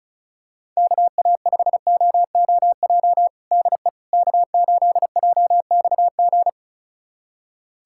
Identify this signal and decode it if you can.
Morse code, heard as a keyed tone, about 36 wpm, 700 Hz. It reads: KA5OOJ DE K8JXG
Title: USA Callsign DE Callsign 35wpm